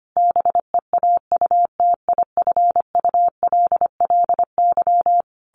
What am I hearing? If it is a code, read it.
Morse code, heard as a keyed tone, about 25 wpm, 700 Hz. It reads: BEAUTIFULLY
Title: A Sound Of Thunder 25wpm